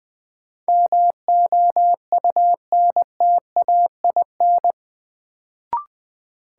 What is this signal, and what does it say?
Morse code, heard as a keyed tone, about 20 wpm, 700 Hz. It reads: MOUNTAIN
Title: Sentences from Top 1300 Words - New Word Drill Alt - 20wpm